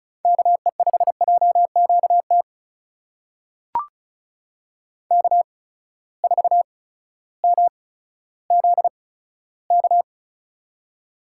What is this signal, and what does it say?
Morse code, heard as a keyed tone, about 36 wpm, spaced as for 16 wpm, 700 Hz. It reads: KE5JQT K4MZK
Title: USA Call Signs - 8x Character Spacing - 35wpm